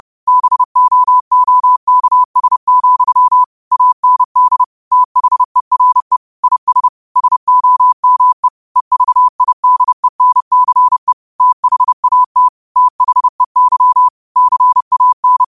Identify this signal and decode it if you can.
Morse code, heard as a keyed tone, about 30 wpm, 1000 Hz. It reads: BOOKS, AND THERE IS SOME EVIDENCE THAT THEY CAN
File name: qotdfiles / 30wpm